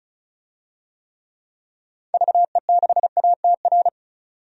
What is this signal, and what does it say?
Morse code, heard as a keyed tone, about 36 wpm, 700 Hz. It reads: VE6ATR
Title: DXCC-100 Call Signs - 6 Characters - 35wpm